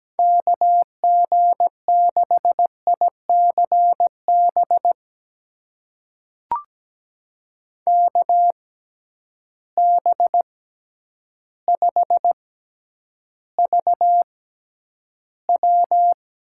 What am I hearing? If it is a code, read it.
Morse code, heard as a keyed tone, about 17 wpm, spaced as for 9 wpm, 700 Hz. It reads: KG6ICB KB5VW
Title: USA Call Signs - 6x Character Spacing - 17wpm